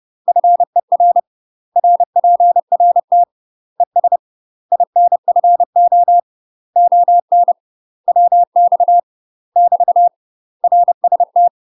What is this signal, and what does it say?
Morse code, heard as a keyed tone, about 30 wpm, 700 Hz. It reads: FER RPRT ES INFO ON WX <BT> RST
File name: UK MASTER QSOs TRAINING / 30WPM / UK MASTER QSOs 0035